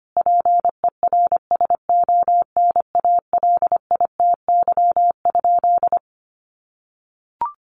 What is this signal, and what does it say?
Morse code, heard as a keyed tone, about 25 wpm, 700 Hz. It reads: PERSONALITY?
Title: Sentences from Top 1400 Words 25wpm